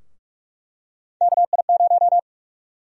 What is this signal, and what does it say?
Morse code, heard as a keyed tone, about 46 wpm, 700 Hz. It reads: KI0
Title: USA Call Sign Prefixes - 45wpm